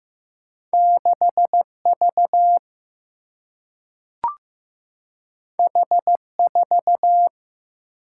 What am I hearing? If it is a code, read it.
Morse code, heard as a keyed tone, about 15 wpm, 700 Hz. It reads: 6V H4
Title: Single-Letter Single-Number 15wpm